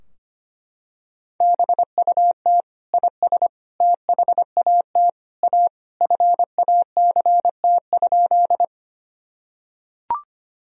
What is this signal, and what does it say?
Morse code, heard as a keyed tone, about 25 wpm, 700 Hz. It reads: BUT IS THAT A FACT?
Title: Sentences from Top 1000 Words - Review Part 3 - 25wpm